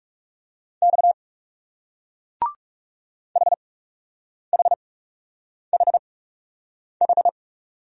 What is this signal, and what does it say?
Morse code, heard as a keyed tone, about 39 wpm, spaced as for 8 wpm, 700 Hz. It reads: X HHHH